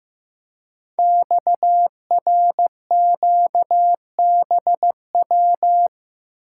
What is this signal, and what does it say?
Morse code, heard as a keyed tone, about 15 wpm, 700 Hz. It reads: XRQBW
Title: Code Groups - 5 Letters - 15wpm